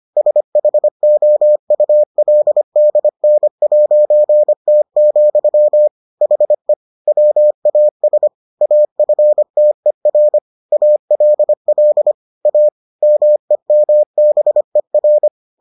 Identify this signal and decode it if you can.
Morse code, heard as a keyed tone, about 25 wpm, 600 Hz. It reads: SHOULDN'T, HE WAS AFTER ALL A MEMBER